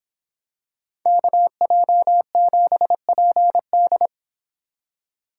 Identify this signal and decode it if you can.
Morse code, heard as a keyed tone, about 26 wpm, 700 Hz. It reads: KJ7PD